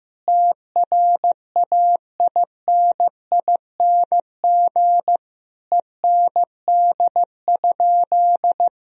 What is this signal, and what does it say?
Morse code, heard as a keyed tone, about 16 wpm, 700 Hz. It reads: TRAINING END?
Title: Sentences from Top 1000 Words - Review Part 4 - 15wpm